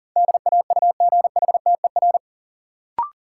Morse code, heard as a keyed tone, about 39 wpm, 700 Hz. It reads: DAUGHTER
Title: Sentences from Top 1500 Words - New Word Drill - 40wpm